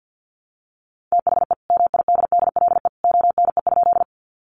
Morse code, heard as a keyed tone, about 49 wpm, 700 Hz. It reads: THE MIDDLE ONE?
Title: Sentences from Top 900 Words 50wpm